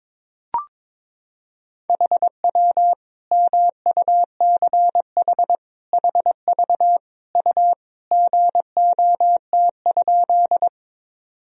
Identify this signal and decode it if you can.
Morse code, heard as a keyed tone, about 22 wpm, 700 Hz. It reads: HW MUCH HV U GOT?
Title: Sentences from Top 300 Words and Abbreviations 22wpm